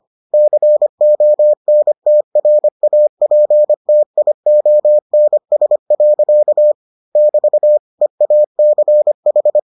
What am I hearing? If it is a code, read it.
Morse code, heard as a keyed tone, about 25 wpm, 600 Hz. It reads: CONTRAPTIONS. <BT> EACH